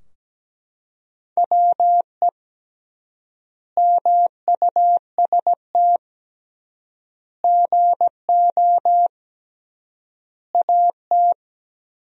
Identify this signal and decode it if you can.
Morse code, heard as a keyed tone, about 17 wpm, 700 Hz. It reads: WE MUST GO AT
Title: 3x Word-Spacing - Sentences from Top 300 Words - 17wpm